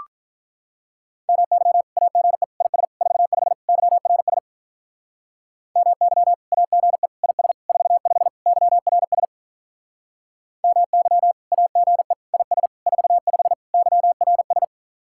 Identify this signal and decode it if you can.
Morse code, heard as a keyed tone, about 45 wpm, 700 Hz. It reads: MY AGE IS 45 YRS MY AGE IS 45 YRS MY AGE IS 45 YRS